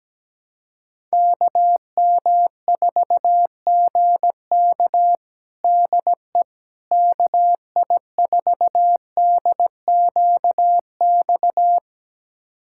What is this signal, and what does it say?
Morse code, heard as a keyed tone, about 17 wpm, 700 Hz. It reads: KM4GK DE KI4DQX
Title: USA Callsign DE Callsign 17wpm